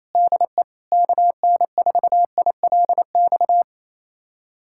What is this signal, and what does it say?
Morse code, heard as a keyed tone, about 28 wpm, 700 Hz. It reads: DE KN4ILX
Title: USA Callsign DE Callsign - 6 Characters - 28wpm